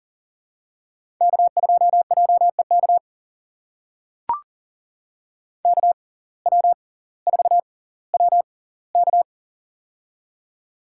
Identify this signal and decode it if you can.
Morse code, heard as a keyed tone, about 40 wpm, spaced as for 20 wpm, 700 Hz. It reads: K2JEK KW4WK
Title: USA Call Signs - 6x Character Spacing - 40wpm